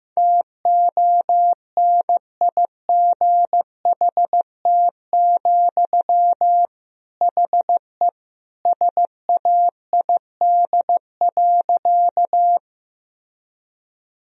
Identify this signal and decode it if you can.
Morse code, heard as a keyed tone, about 15 wpm, 700 Hz. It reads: TONIGHT, HE SAID.
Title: Sentences from Top 1000 Words - Review Part 3 - 15wpm